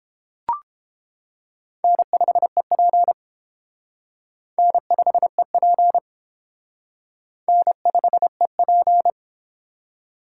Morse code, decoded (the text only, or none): N5EP N5EP N5EP